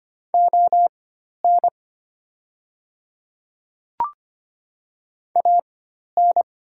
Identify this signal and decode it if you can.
Morse code, heard as a keyed tone, about 25 wpm, spaced as for 12 wpm, 700 Hz. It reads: ON AN
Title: 4x Character Spacing - Top 100 Words - 25wpm